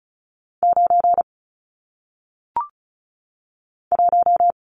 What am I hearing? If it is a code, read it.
Morse code, heard as a keyed tone, about 36 wpm, 700 Hz. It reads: 9 1
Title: Numbers - 1-Digit 35wpm